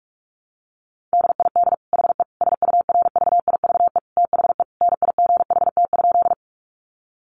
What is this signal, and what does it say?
Morse code, heard as a keyed tone, about 44 wpm, 700 Hz. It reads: DID HE SURVIVE THE NIGHT?
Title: Sentences from Top 1300 Words 45wpm